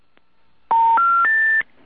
Special information tone.